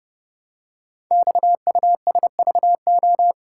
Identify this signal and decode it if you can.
Morse code, heard as a keyed tone, about 28 wpm, 700 Hz. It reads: XUSVO
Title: Code Groups - 5 Letter:Numbers - 30wpm